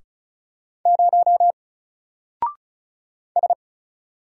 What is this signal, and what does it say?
Morse code, heard as a keyed tone, about 34 wpm, 700 Hz. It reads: 0 S
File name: Single-Letter-Number-35wpm